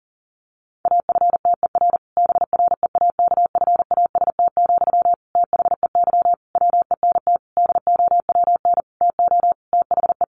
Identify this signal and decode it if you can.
Morse code, heard as a keyed tone, about 40 wpm, 700 Hz. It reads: AFTER BREAKFAST, THEY WENT DOWN TO THE
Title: Sentences from Top 1500 Words 40wpm